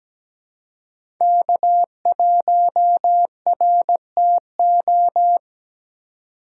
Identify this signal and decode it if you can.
Morse code, heard as a keyed tone, about 17 wpm, 700 Hz. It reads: K1RTO
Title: USA Call Signs - 5 Characters - 2x Character Spacing - 17wpm